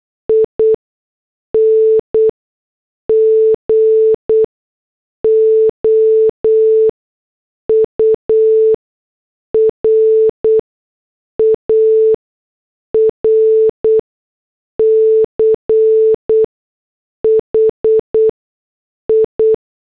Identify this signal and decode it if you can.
Morse code, heard as a keyed tone, about 8 wpm, spaced as for 6 wpm, 440 Hz. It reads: ING OUR ARCHI
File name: morse / headlines